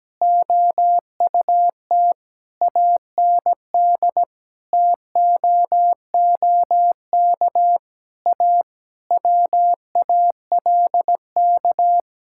Morse code, heard as a keyed tone, about 17 wpm, 700 Hz. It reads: OUT AND TOOK A WALK
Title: Sentences from Top 1600 Words 17wpm